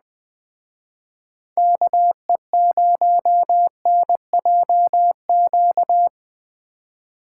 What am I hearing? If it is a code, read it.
Morse code, heard as a keyed tone, about 21 wpm, 700 Hz. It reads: KE0NJQ